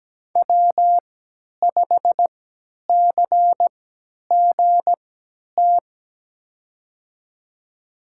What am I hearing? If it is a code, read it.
Morse code, heard as a keyed tone, about 17 wpm, spaced as for 10 wpm, 700 Hz. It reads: W5CGT
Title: USA Call Signs - 5 Characters - 3x Character Spacing - 17wpm